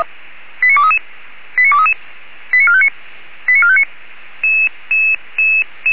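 Five-tone selective call (ZVEI), heard as quick runs of five tones, tones of about 70 ms. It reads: 89233 89233 89459 89459
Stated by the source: ZVEI five-tone selective call received by SDR.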